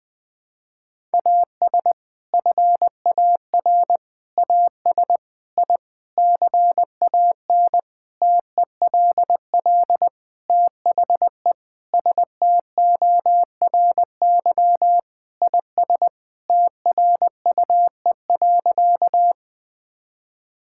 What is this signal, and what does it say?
Morse code, heard as a keyed tone, about 20 wpm, 700 Hz. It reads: AS FAR AS I CAN TELL THE STORY IS TRUE.
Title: Sentences with 7-Word Phrases - 20wpm